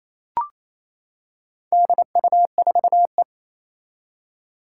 Morse code, heard as a keyed tone, about 28 wpm, 700 Hz. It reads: DU4E
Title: Four Challenging Letter-Number Combinations 28wpm